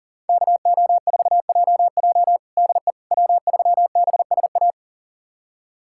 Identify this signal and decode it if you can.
Morse code, heard as a keyed tone, about 40 wpm, 700 Hz. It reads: KO4JJ DE W3BSA
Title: USA Callsign DE Callsign - 5 Characters - 40wpm